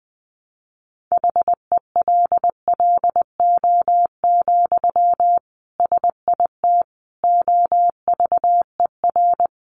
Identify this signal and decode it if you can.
Morse code, heard as a keyed tone, about 20 wpm, 700 Hz. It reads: HELLO, SIT OVER
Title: Sentences from Top 1100 Words 20wpm